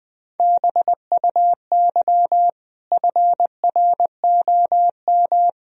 Morse code, heard as a keyed tone, about 20 wpm, 700 Hz. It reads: BUY FROM
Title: Sentences from Top 1000 Words - Review Part 4 - 20wpm